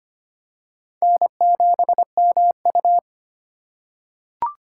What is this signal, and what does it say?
Morse code, heard as a keyed tone, about 25 wpm, 700 Hz. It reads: N7MU
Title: USA Call Signs - 4 Characters - 2x Character Spacing - 25wpm